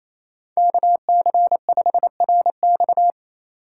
Morse code, heard as a keyed tone, about 33 wpm, 700 Hz. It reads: KC5RX
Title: Speed Racing - USA 2x2 Call Signs 28wpm